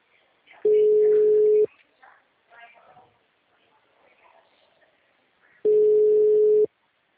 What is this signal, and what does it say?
Ringback tone (Europe, 425 Hz), with speech.
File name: out- -2002- - - .39631